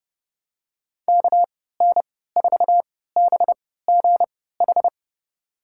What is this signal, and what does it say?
Morse code, heard as a keyed tone, about 32 wpm, spaced as for 17 wpm, 700 Hz. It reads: KN4BGH
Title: USA Call Signs - 6 Characters - 3x Character Spacing - 30wpm